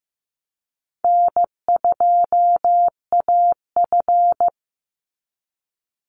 Morse code, heard as a keyed tone, about 15 wpm, 700 Hz. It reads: N2AF
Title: NA SOTA Chaser Callsigns 15wpm